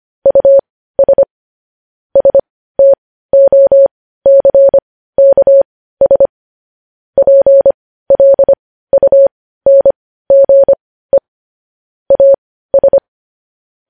Morse code, heard as a keyed tone, about 25 wpm, spaced as for 15 wpm, 550 Hz. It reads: US STOCKS PLUNGE AS
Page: news headlines in morse code at 15 wpm